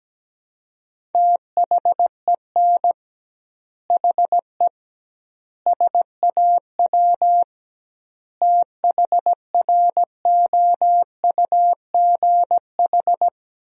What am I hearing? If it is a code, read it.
Morse code, heard as a keyed tone, about 17 wpm, 700 Hz. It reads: THEN HE SAW THROUGH
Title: 2x Word-Spacing - Sentences from Top 400 Words - 17wpm